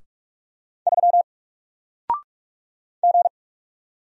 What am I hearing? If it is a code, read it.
Morse code, heard as a keyed tone, about 46 wpm, 700 Hz. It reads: 3 G
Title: Single Character (Letter:Number:Punctuation:Prosign) Alt 45wpm